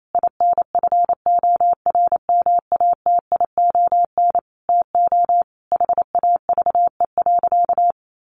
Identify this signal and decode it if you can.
Morse code, heard as a keyed tone, about 28 wpm, 700 Hz. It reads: INFORMATION TO HAVE.
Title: Sentences from Top 1400 Words 28wpm